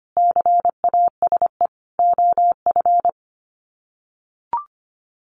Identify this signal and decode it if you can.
Morse code, heard as a keyed tone, about 25 wpm, 700 Hz. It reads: CASE OF
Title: Sets of 4 Words 25wpm